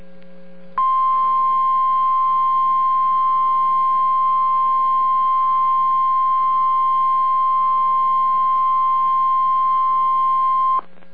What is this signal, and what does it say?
Emergency Alert System (SAME) tones (the 1050 Hz attention tone). It is heard from 0.7 to 10.8 s.